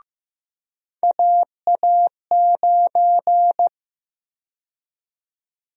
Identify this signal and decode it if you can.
Morse code, heard as a keyed tone, about 15 wpm, 700 Hz. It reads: AA9
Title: USA Call Sign Prefixes - 15wpm